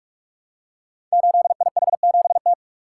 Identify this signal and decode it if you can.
Morse code, heard as a keyed tone, about 44 wpm, 700 Hz. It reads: 8IH7T
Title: Code Groups - 5 Letter:Numbers - 45wpm